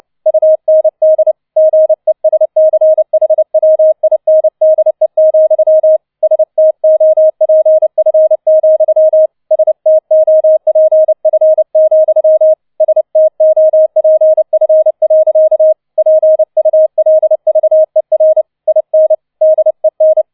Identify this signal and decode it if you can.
Morse code, heard as a keyed tone, about 29 wpm, 630 Hz. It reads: UND GESCHWINDE, STOPF, STOPF, STOPF. PULVER IN DEN